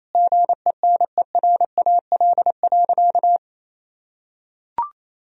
Morse code, heard as a keyed tone, about 28 wpm, 700 Hz. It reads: GENERAL.
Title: Sentences from Top 1000 Words - Review Part 3 - 28wpm